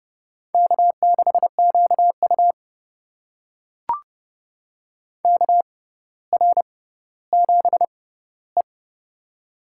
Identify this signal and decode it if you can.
Morse code, heard as a keyed tone, about 30 wpm, spaced as for 15 wpm, 700 Hz. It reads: K6QU KR7E
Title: USA Call Signs - 4 Characters - 6x Character Spacing - 30wpm